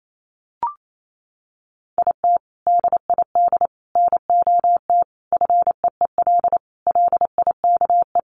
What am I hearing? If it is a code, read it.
Morse code, heard as a keyed tone, about 28 wpm, 700 Hz. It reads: IT DID NOT FEEL LIKE
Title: Sentences from Top 1100 Words 28wpm